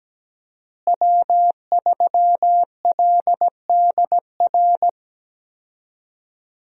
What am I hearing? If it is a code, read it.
Morse code, heard as a keyed tone, about 17 wpm, 700 Hz. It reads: W3LDR
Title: USA Call Signs - 5 Characters - Repeated 3x - 17wpm